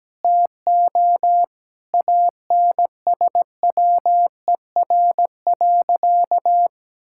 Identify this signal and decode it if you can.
Morse code, heard as a keyed tone, about 17 wpm, 700 Hz. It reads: TO ANSWER.
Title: Sentences from Top 1600 Words 17wpm